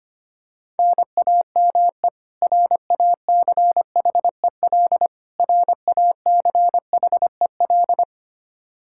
Morse code, heard as a keyed tone, about 25 wpm, 700 Hz. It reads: NAME RACHEL RACHEL
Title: Practice QSO - Exchange 1 - Beginning - 25wpm